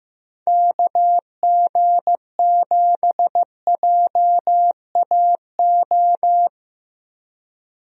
Morse code, heard as a keyed tone, about 15 wpm, 700 Hz. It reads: KG7JAO
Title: USA Call Signs - 6 Characters - Repeated 3x - 15wpm